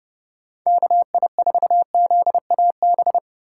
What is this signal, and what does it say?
Morse code, heard as a keyed tone, about 40 wpm, 700 Hz. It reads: KI4ZAB